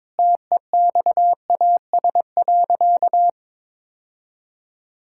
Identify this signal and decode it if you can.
Morse code, heard as a keyed tone, about 21 wpm, 700 Hz. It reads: TEXAS.
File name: practice / 22wpm / Sentences with 4-Word Phrases - 22wpm